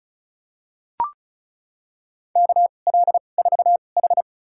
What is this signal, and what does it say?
Morse code, heard as a keyed tone, about 34 wpm, spaced as for 25 wpm, 700 Hz. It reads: KL4H
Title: USA Call Signs - 4 Characters - 2x Character Spacing - 35wpm